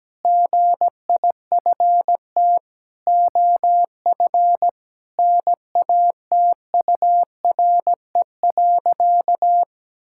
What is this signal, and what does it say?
Morse code, heard as a keyed tone, about 17 wpm, 700 Hz. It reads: GIFT OF NATURE.
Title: Sentences from Top 1000 Words - Review Part 6 - 17wpm